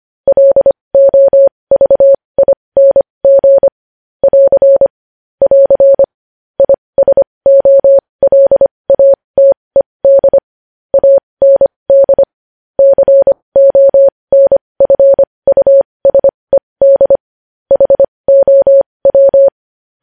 Morse code, heard as a keyed tone, about 25 wpm, spaced as for 20 wpm, 550 Hz. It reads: LOVING <AR> <AR> ISOLATED AND CONFUSED HOW
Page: News Headlines in Morse Code at 20 WPM